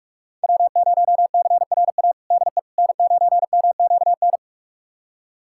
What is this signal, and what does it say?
Morse code, heard as a keyed tone, about 43 wpm, 700 Hz. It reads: W0CRA DE N9MQN
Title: USA Callsign DE Callsign - 5 Characters - 45wpm